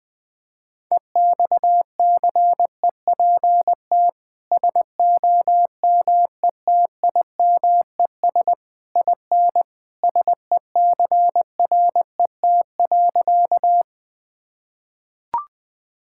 Morse code, heard as a keyed tone, about 20 wpm, 700 Hz. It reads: EXCEPT SOMETIMES IN SECRET.
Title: Sentences from Top 1600 Words 20wpm